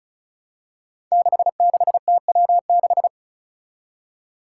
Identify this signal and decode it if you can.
Morse code, heard as a keyed tone, about 36 wpm, 700 Hz. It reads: 66TW6
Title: Five Challenging Letter-Number Combinations 35wpm